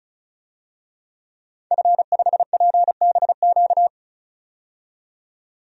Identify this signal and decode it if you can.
Morse code, heard as a keyed tone, about 34 wpm, 700 Hz. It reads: F5PBQ
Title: DXCC-100 Call Signs - 5 Characters - 35wpm